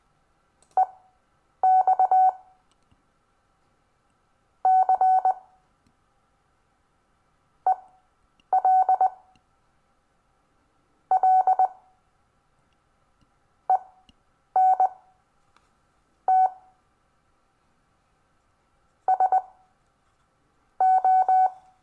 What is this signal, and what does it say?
Morse code, heard as a keyed tone, about 19 wpm, spaced as for 5 wpm, 750 Hz. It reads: EXCELLENT SO
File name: cw / Lesson26 SUMMARY